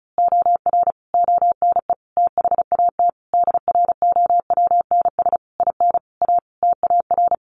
Morse code, heard as a keyed tone, about 34 wpm, 700 Hz. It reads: OR ONE THAT DROWNS IN A TAR
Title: A Sound Of Thunder 35wpm